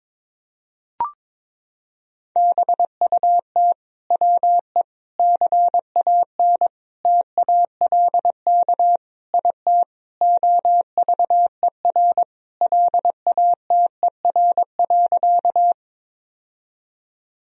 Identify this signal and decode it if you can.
Morse code, heard as a keyed tone, about 21 wpm, 700 Hz. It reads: BUT WE CAN TALK IT OVER LATER.
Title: Sentences from Top 1000 Words - Review Part 4 - 22wpm